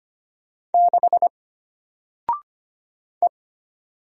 Morse code, heard as a keyed tone, about 25 wpm, 700 Hz. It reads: 6 E
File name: Single-Letter-Number-25wpm